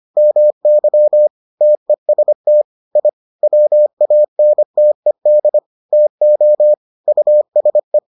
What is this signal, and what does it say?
Morse code, heard as a keyed tone, about 25 wpm, 600 Hz. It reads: MY TEST I WANTED TO USE